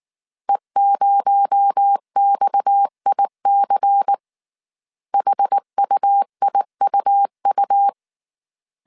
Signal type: Morse code